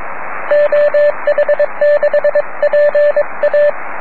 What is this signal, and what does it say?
Morse code, heard as a keyed tone, about 22 wpm, 590 Hz, with radio noise. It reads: OH6PA
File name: om3pv 70mhz / zaznam / oh1nd oh6pa